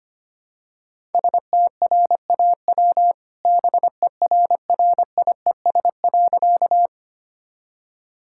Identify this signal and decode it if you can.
Morse code, heard as a keyed tone, about 25 wpm, 700 Hz. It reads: STRAW BERRIES.
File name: practice / 25wpm / All-Age Appropriate Jokes - 25wpm